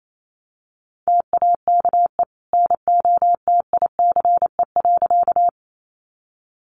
Morse code, heard as a keyed tone, about 28 wpm, 700 Hz. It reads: TAKE NOTICE.
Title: Sentences from Top 400 Words 28wpm